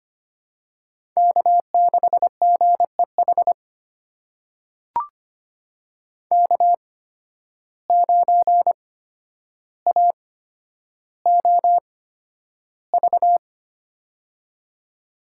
Morse code, heard as a keyed tone, about 25 wpm, spaced as for 11 wpm, 700 Hz. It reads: K6GEH K9AOV
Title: USA Call Signs - 5 Characters - 8x Character Spacing - 25wpm